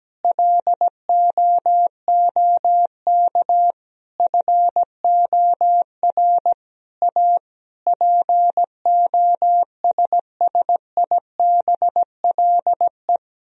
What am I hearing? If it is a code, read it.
Morse code, heard as a keyed tone, about 17 wpm, 700 Hz. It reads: LOOK FOR A POSSIBLE
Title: Sentences from Top 1000 Words - Review Part 4 - 17wpm